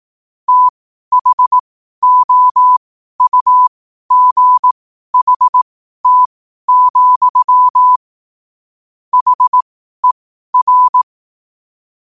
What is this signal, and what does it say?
Morse code, heard as a keyed tone, about 18 wpm, spaced as for 12 wpm, 1000 Hz. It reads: THOUGHT, HER
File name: qotdfiles / 12wpm